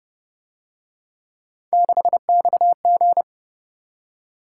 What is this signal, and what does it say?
Morse code, heard as a keyed tone, about 31 wpm, 700 Hz. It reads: 6XG